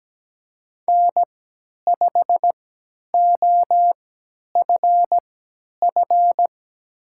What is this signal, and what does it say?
Morse code, heard as a keyed tone, about 17 wpm, spaced as for 10 wpm, 700 Hz. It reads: N5OFF
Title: USA Call Signs - 5 Characters - 3x Character Spacing - 17wpm